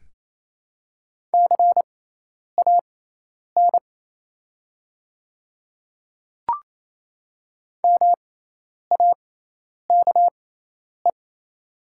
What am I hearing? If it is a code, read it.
Morse code, heard as a keyed tone, about 28 wpm, spaced as for 10 wpm, 700 Hz. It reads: CAN MAKE
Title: 6x Character Spacing - Top 100 Words - 28wpm